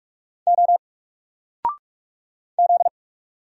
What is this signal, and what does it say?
Morse code, heard as a keyed tone, about 44 wpm, 700 Hz. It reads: O Z
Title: Single Letter-Number Alt 45wpm